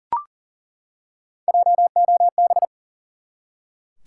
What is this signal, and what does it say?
Morse code, heard as a keyed tone, about 40 wpm, 700 Hz. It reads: JOB
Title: Three Letter Words - 40wpm